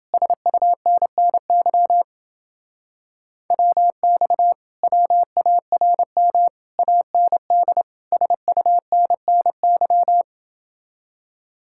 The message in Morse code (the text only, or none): SUNNY WX WARM AND SUNNY